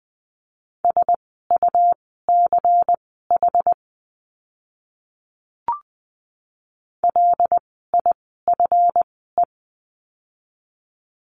Morse code, heard as a keyed tone, about 20 wpm, spaced as for 14 wpm, 700 Hz. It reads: SUCH LIFE
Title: 230 Lesson 37f - Letter x - Words 20wpm